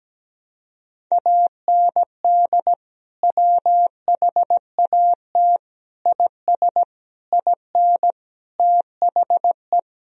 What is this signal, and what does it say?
Morse code, heard as a keyed tone, about 17 wpm, 700 Hz. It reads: AND WHAT IS IN THE
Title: Sentences from Top 1000 Words - Review Part 1 - 17wpm